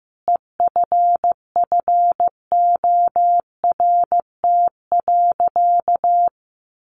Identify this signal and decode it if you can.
Morse code, heard as a keyed tone, about 14 wpm, 700 Hz. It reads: EFFORT.
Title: Sentences from Top 1200 Words 15wpm